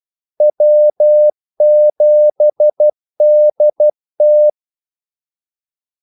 Morse code, heard as a keyed tone, about 12 wpm, 600 Hz. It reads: W7DT
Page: 2 ekstra ordmellemrum mellem kaldesignalerne